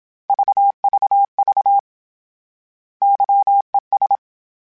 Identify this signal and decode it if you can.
Morse code, heard as a keyed tone, about 27 wpm, 800 Hz. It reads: VVV YES